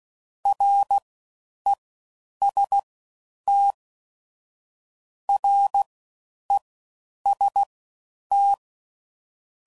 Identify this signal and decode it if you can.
Morse code, heard as a keyed tone, about 16 wpm, spaced as for 9 wpm, 800 Hz. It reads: REST REST